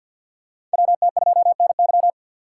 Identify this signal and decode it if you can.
Morse code, heard as a keyed tone, about 54 wpm, 700 Hz. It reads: WT2NY
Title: USA Call Signs - 5 Characters - 8x Character Spacing - 50wpm